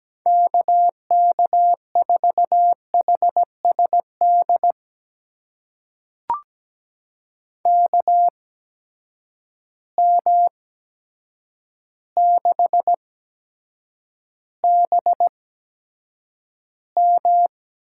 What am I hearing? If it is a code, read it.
Morse code, heard as a keyed tone, about 17 wpm, spaced as for 8 wpm, 700 Hz. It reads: KK4HSD KM6BM
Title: USA Call Signs - 8x Character Spacing - 17wpm